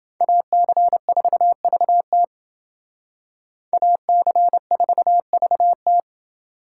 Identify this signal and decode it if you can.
Morse code, heard as a keyed tone, about 28 wpm, 700 Hz. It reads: AC4VT AC4VT